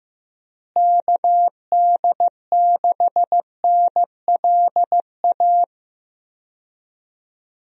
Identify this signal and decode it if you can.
Morse code, heard as a keyed tone, about 15 wpm, 700 Hz. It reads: KD6NLA
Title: USA Call Signs - 6 Characters - Repeated 2x - 15wpm